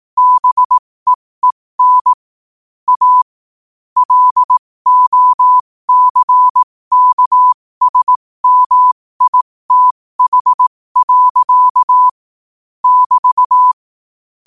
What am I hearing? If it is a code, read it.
Morse code, heard as a keyed tone, about 18 wpm, spaced as for 15 wpm, 1000 Hz. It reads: BEEN A LOCKSMITH. <BT>
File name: qotdfiles / 15wpm